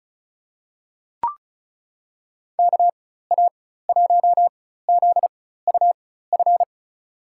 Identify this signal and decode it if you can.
Morse code, heard as a keyed tone, about 36 wpm, spaced as for 16 wpm, 700 Hz. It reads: KA1ZUF
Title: USA Call Signs - 6 Characters - 4x Character Spacing - 35wpm